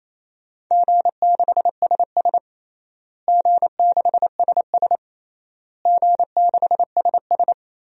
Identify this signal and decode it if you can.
Morse code, heard as a keyed tone, about 28 wpm, 700 Hz. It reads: G6SS G6SS G6SS
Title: DXCC-100 Call Signs - 4 Characters - Repeated 3x - 28wpm